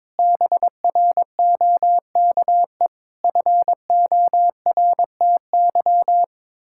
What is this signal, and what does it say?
Morse code, heard as a keyed tone, about 22 wpm, 700 Hz. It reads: BROKE FORTY
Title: Pangram Challenge v2 - 22wpm